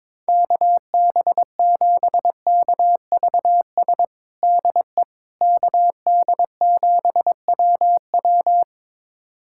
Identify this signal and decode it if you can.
Morse code, heard as a keyed tone, about 22 wpm, 700 Hz. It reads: KB7KVS DE KD7WW
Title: USA Callsign DE Callsign 22wpm